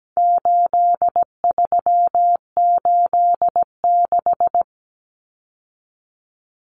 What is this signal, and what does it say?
Morse code, heard as a keyed tone, about 17 wpm, 700 Hz. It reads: 8386
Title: Numbers - 4-Digit 17wpm